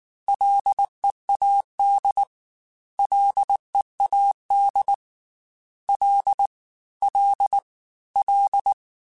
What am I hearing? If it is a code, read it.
Morse code, heard as a keyed tone, about 19 wpm, 800 Hz. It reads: LEAD LEAD L L L
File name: F and L / F and L mp3 / F and L Prefixes 3x2 Twice @18